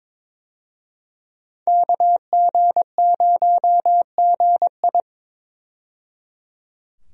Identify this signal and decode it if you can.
Morse code, heard as a keyed tone, about 22 wpm, 700 Hz. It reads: KG0GI
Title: USA Call Signs - 5 Characters - 22wpm